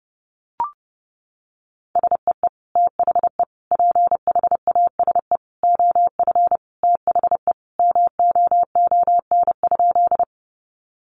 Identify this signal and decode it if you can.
Morse code, heard as a keyed tone, about 28 wpm, 700 Hz. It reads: SEE THE PHASE OF THE MOON?
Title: Sentences from Top 1200 Words 30wpm